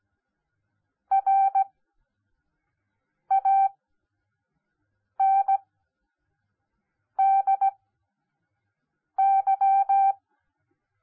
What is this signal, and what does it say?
Morse code, heard as a keyed tone, about 17 wpm, spaced as for 5 wpm, 780 Hz. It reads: RANDY
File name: Onlinelehrgang / morsen / DJ4UF Morsen10